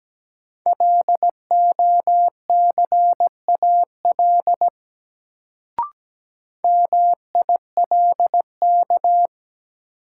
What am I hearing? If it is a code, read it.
Morse code, heard as a keyed tone, about 17 wpm, 700 Hz. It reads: LOCAL MILK
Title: Sentences from Top 1500 Words - New Word Drill Alt - 17wpm